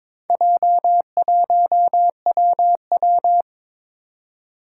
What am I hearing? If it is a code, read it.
Morse code, heard as a keyed tone, about 23 wpm, 700 Hz. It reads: J1WW
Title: Four Challenging Letter-Number Combinations 22wpm